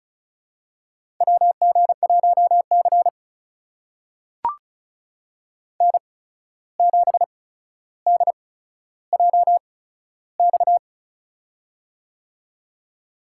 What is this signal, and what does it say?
Morse code, heard as a keyed tone, about 34 wpm, spaced as for 14 wpm, 700 Hz. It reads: WG1C N7DJX
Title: USA Call Signs - 8x Character Spacing - 35wpm